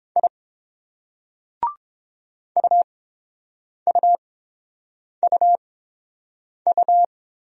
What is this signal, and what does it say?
Morse code, heard as a keyed tone, about 30 wpm, spaced as for 7 wpm, 700 Hz. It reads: I UUUU